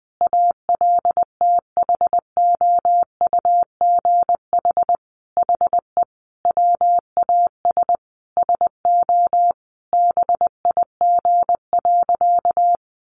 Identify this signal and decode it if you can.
Morse code, heard as a keyed tone, about 20 wpm, 700 Hz. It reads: ALTHOUGH HE WAS SO BIG.
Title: Sentences from Top 600 Words 20wpm